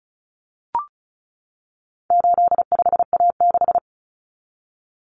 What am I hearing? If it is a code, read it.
Morse code, heard as a keyed tone, about 34 wpm, 700 Hz. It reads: 85A6
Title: Common Cut Numbers - 4-Digit 35wpm